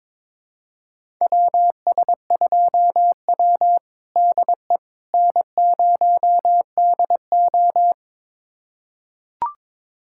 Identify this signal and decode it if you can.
Morse code, heard as a keyed tone, about 22 wpm, 700 Hz. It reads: WS2W DE N0DO
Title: USA Callsign DE Callsign - 4 Characters - 22wpm